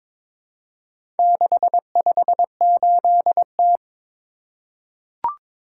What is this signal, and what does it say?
Morse code, heard as a keyed tone, about 23 wpm, 700 Hz. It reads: 658T
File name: practice / 22wpm / Four Letter-Number Combinations - 22wpm